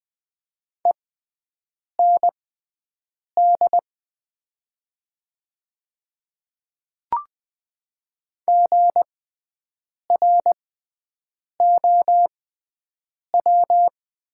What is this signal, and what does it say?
Morse code, heard as a keyed tone, about 20 wpm, spaced as for 7 wpm, 700 Hz. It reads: END GROW
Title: 6x Character Spacing - Top 300 Words - 20wpm